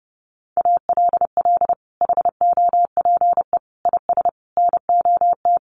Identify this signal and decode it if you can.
Morse code, heard as a keyed tone, about 30 wpm, 700 Hz. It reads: ALL HOPE IS NOT
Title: Sentences from Top 1400 Words 30wpm